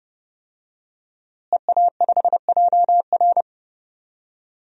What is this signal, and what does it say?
Morse code, heard as a keyed tone, about 30 wpm, 700 Hz. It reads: EA5JR
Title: DXCC-100 Call Signs - 5 Characters - Repeated 2x - 30wpm